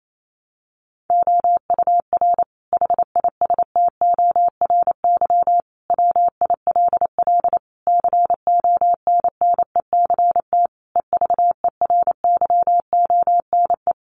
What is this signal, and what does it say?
Morse code, heard as a keyed tone, about 28 wpm, 700 Hz. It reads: OUR HISTORY WILL CONNECT EVERYONE
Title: Sentences from Top 1500 Words 28wpm